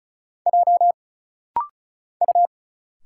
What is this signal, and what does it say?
Morse code, heard as a keyed tone, about 34 wpm, 700 Hz. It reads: J U
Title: Single Letters - Rapid-Fire 35wpm